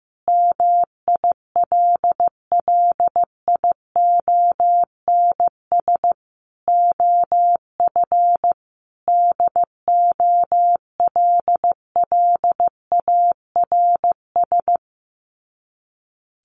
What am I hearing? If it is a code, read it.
Morse code, heard as a keyed tone, about 15 wpm, 700 Hz. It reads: MILLIONS OF DOLLARS
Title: Sets of 5 Words Encore 15wpm